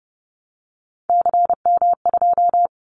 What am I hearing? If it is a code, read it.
Morse code, heard as a keyed tone, about 30 wpm, 700 Hz. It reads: CM2
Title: Callsign Prefixes for Top 100 DXCC - 30wpm